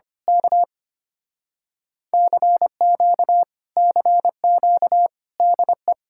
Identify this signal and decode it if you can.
Morse code, heard as a keyed tone, about 24 wpm, 700 Hz. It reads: K CQ CQ DE
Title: Speed-Racing Common QSO Phrases - 25wpm